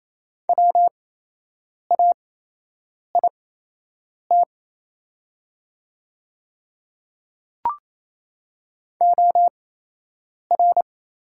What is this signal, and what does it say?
Morse code, heard as a keyed tone, about 28 wpm, spaced as for 8 wpm, 700 Hz. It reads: WAIT OR
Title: 8x Character Spacing - Top 500 Words - 28wpm